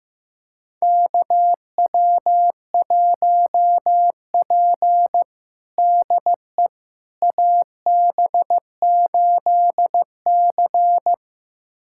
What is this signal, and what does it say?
Morse code, heard as a keyed tone, about 15 wpm, 700 Hz. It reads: KW1P DE AB8C
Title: USA Callsign DE Callsign - 4 Characters - 15wpm